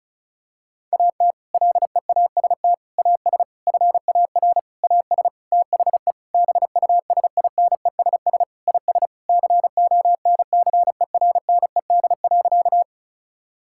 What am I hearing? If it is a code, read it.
Morse code, heard as a keyed tone, about 36 wpm, 700 Hz. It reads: AT LEAST AS FAR AS THE BUSINESS IS CONCERNED.
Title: Sentences from Top 1000 Words - Review Part 6 - 35wpm